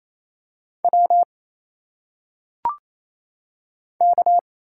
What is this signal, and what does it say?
Morse code, heard as a keyed tone, about 27 wpm, 700 Hz. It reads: W K
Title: Single Letters - Phonetic Pronunciation 28wpm